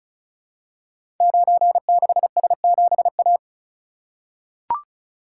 Morse code, heard as a keyed tone, about 36 wpm, 690 Hz. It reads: 96S7A
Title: Code Groups - 5 Letter:Numbers - 35wpm